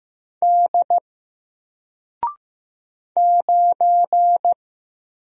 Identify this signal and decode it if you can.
Morse code, heard as a keyed tone, about 15 wpm, 700 Hz. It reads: D 9
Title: Single Letter-Number Alt 15wpm